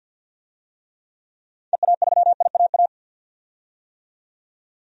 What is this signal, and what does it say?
Morse code, heard as a keyed tone, about 46 wpm, 700 Hz. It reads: EA3IAA
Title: DXCC-100 Call Signs - 6 Characters - 50wpm